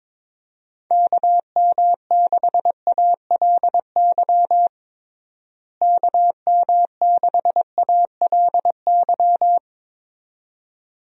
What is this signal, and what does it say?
Morse code, heard as a keyed tone, about 22 wpm, 700 Hz. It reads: KM6ALY KM6ALY
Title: USA Call Signs - 6 Characters - Repeated 2x - 22wpm